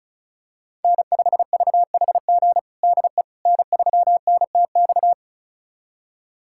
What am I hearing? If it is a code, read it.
Morse code, heard as a keyed tone, about 36 wpm, 700 Hz. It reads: N5VHG DE N3NTX
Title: USA Callsign DE Callsign - 5 Characters - 35wpm